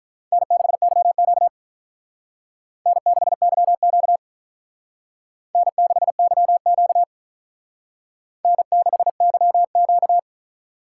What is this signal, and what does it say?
Morse code, heard as a keyed tone, about 44 wpm, 700 Hz. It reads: N6YQ N6YQ N6YQ N6YQ